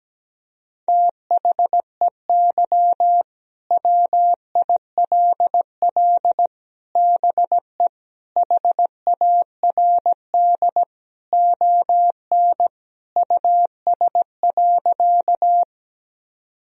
Morse code, heard as a keyed tone, about 17 wpm, 700 Hz. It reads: THEY WILL BE HARD ON US.
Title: Sentences from Top 1000 Words - Review Part 1 - 17wpm